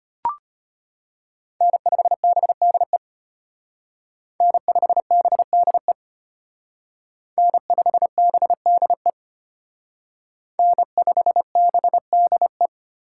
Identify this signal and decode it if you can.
Morse code, heard as a keyed tone, about 32 wpm, 700 Hz. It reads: N5BDE N5BDE N5BDE N5BDE